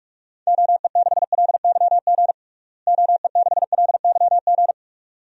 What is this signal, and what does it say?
Morse code, heard as a keyed tone, about 46 wpm, 700 Hz. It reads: OE6LYG OE6LYG
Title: DXCC-100 Call Signs - 6 Characters - Repeated 2x - 45wpm